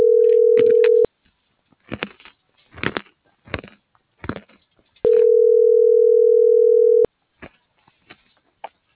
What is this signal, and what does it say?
Ringback tone (North America), 440 + 480 Hz.